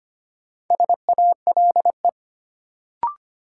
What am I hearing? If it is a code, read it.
Morse code, heard as a keyed tone, about 25 wpm, 700 Hz. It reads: SALE
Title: Sentences from Top 1400 Words - New Word Drill Alt - 25wpm